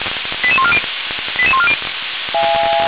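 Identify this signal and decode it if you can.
Five-tone selective call (ZVEI), heard as quick runs of five tones, tones of about 70 ms. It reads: 88140 88140
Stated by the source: ZVEI five-tone selective call received by SDR, logged by Feuerwehr Orlamünde fire brigade.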